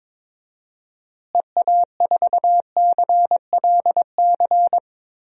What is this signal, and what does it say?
Morse code, heard as a keyed tone, about 22 wpm, 700 Hz. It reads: EA4CLC
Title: DXCC-100 Call Signs - 6 Characters - Repeated 2x - 22wpm